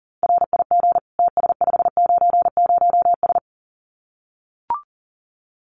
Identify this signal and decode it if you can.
Morse code, heard as a keyed tone, about 40 wpm, 700 Hz. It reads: RIG TS590S
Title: Common QSO Phrases Encore 40wpm